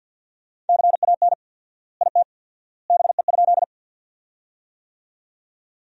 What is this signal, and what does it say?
Morse code, heard as a keyed tone, about 47 wpm, 700 Hz. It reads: CAN IT BE?
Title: 4x Word-Spacing - Sentences from Top 100 Words - 50wpm